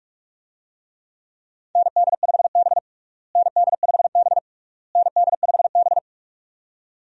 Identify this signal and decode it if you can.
Morse code, heard as a keyed tone, about 46 wpm, 700 Hz. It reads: ND5B ND5B ND5B
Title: USA Call Signs - 4 Characters - Repeated 3x - 45wpm